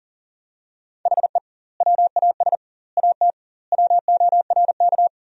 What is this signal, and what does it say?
Morse code, heard as a keyed tone, about 40 wpm, 700 Hz. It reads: HE WAS AT WORK
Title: 2x Word-Spacing - Sentences from Top 200 Words - 40wpm